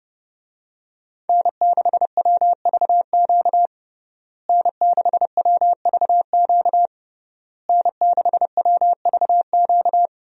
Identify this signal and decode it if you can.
Morse code, heard as a keyed tone, about 30 wpm, 700 Hz. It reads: N6WVQ N6WVQ N6WVQ
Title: USA Call Signs - 5 Characters - Repeated 3x - 30wpm